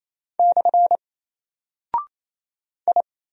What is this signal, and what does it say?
Morse code, heard as a keyed tone, about 27 wpm, 700 Hz. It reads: / I
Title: Single Character (Letter:Number:Punctuation:Prosign) Alt 28wpm